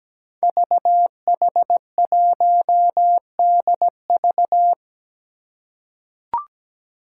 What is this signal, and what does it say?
Morse code, heard as a keyed tone, about 17 wpm, 700 Hz. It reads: VH1DV
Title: Five Challenging Letter-Number Combinations 17wpm